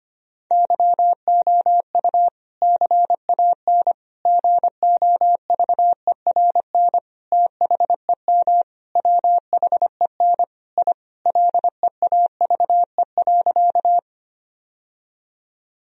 Morse code, heard as a keyed tone, about 25 wpm, 700 Hz. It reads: YOU CAN GOVERN THEM WHEN I LEAVE.
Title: Sentences from Top 1000 Words - Review Part 2 - 25wpm